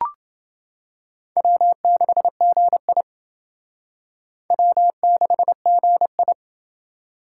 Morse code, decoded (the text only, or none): W6GI W6GI